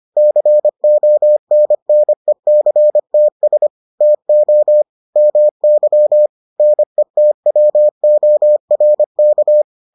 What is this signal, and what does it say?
Morse code, heard as a keyed tone, about 25 wpm, 600 Hz. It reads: CONNECTS TO MY NETWORK